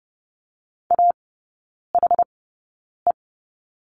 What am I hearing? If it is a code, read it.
Morse code, heard as a keyed tone, about 30 wpm, spaced as for 9 wpm, 700 Hz. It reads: AHE
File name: Lesson 15e - Number 5 - Three Characters 30wpm